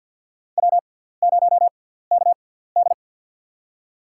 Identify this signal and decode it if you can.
Morse code, heard as a keyed tone, about 52 wpm, spaced as for 17 wpm, 700 Hz. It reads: W0KD
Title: USA Call Signs - 4 Characters - 6x Character Spacing - 50wpm